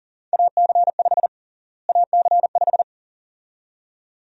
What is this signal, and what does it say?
Morse code, heard as a keyed tone, about 40 wpm, 700 Hz. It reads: AC5 AC5
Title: USA Call Sign Prefixes - Repeated 2x - 40wpm